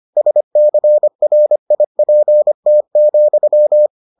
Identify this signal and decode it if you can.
Morse code, heard as a keyed tone, about 25 wpm, 600 Hz. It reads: SCRIPT,